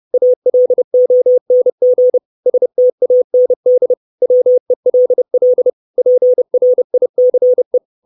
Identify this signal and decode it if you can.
Morse code, heard as a keyed tone, about 30 wpm, 500 Hz. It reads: ALONG STAND WELL PRICE
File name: 632 Words 30WPM